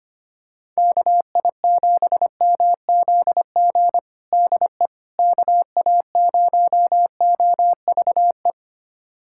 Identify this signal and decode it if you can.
Morse code, heard as a keyed tone, about 25 wpm, 700 Hz. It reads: KI7MZG DE KA0OVE
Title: USA Callsign DE Callsign 25wpm